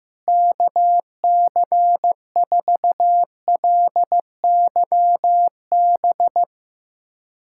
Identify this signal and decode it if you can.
Morse code, heard as a keyed tone, about 15 wpm, 700 Hz. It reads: KC4LYB
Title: USA Callsign DE Callsign - 6 Characters - 15wpm